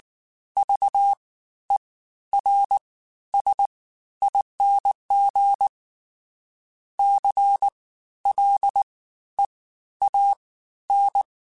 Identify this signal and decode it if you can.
Morse code, heard as a keyed tone, about 19 wpm, spaced as for 12 wpm, 800 Hz. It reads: VERSING CLEAN
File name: Special INGs-mp3s / Special INGs-mp3s / Fast ing @18